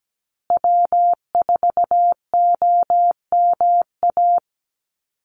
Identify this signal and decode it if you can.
Morse code, heard as a keyed tone, about 17 wpm, 700 Hz. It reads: W4OMA
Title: USA Call Signs Repeated 3x 17wpm